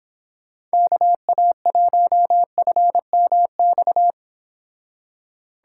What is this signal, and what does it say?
Morse code, heard as a keyed tone, about 29 wpm, 700 Hz. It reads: KA1FMX